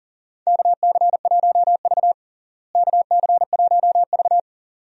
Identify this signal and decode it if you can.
Morse code, heard as a keyed tone, about 40 wpm, 700 Hz. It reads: KC1V KC1V
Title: USA Call Signs - 4 Characters - Repeated 2x - 40wpm